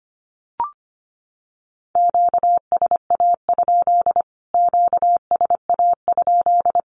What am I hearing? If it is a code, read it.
Morse code, heard as a keyed tone, about 25 wpm, 700 Hz. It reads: QSA? QSA?
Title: Q-Codes Repeated 3x 25wpm